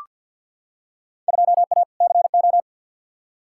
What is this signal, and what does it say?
Morse code, heard as a keyed tone, about 47 wpm, 700 Hz. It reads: 2A CO
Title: Field Day Exchanges - 50wpm